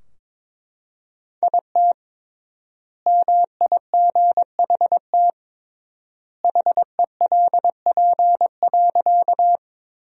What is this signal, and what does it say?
Morse code, heard as a keyed tone, about 22 wpm, 700 Hz. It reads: IT MIGHT HELP.
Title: 3x Word-Spacing - Sentences from Top 300 Words - 22wpm